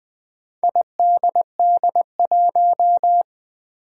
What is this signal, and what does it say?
Morse code, heard as a keyed tone, about 20 wpm, 700 Hz. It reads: IDD1
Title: Four Challenging Letter-Number Combinations 20wpm